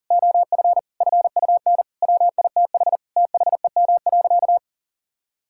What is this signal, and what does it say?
Morse code, heard as a keyed tone, about 40 wpm, 700 Hz. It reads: OF FUN WITH THEM.
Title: Sentences from Top 1000 Words - Review Part 5 - 40wpm